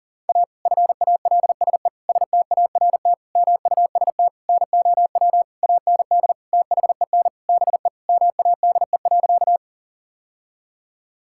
Morse code, heard as a keyed tone, about 40 wpm, 700 Hz. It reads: A FALSE START MUST NOW AND THEN BE MADE.
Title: Sentences from Top 1600 Words 40wpm